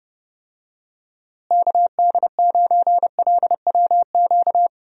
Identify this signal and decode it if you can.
Morse code, heard as a keyed tone, about 31 wpm, 700 Hz. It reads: KD9LWQ
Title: USA Call Signs - 6 Characters - Repeated 3x - 30wpm